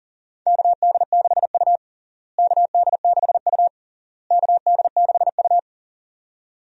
Morse code, heard as a keyed tone, about 37 wpm, 700 Hz. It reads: KD6U KD6U KD6U
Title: USA Call Signs - 4 Characters - Repeated 3x - 40wpm